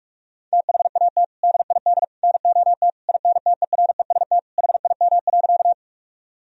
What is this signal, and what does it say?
Morse code, heard as a keyed tone, about 46 wpm, 700 Hz. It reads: THAT DID NOT INTEREST HIM.
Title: Sentences from Top 1000 Words - Review Part 2 - 45wpm